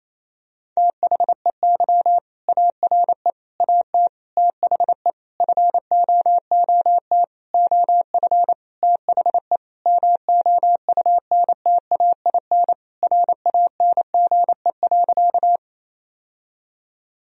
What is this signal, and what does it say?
Morse code, heard as a keyed tone, about 28 wpm, 700 Hz. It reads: THEY ARE AT THE FOOT OF THE MOUNTAIN RANGE.
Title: Sentences from Top 1000 Words - Review Part 4 - 28wpm